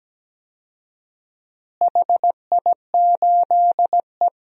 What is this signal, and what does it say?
Morse code, heard as a keyed tone, about 17 wpm, 700 Hz. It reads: HI8E
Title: DXCC-100 Call Signs - 4 Characters - Repeated 3x - 17wpm